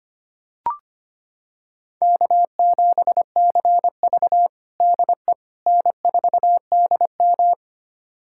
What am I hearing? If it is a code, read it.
Morse code, heard as a keyed tone, about 25 wpm, 700 Hz. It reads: K7CV DE N4DM
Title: USA Callsign DE Callsign - 4 Characters - 25wpm